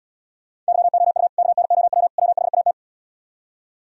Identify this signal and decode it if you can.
Morse code, heard as a keyed tone, about 77 wpm, 700 Hz. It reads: YOU KNOW WHAT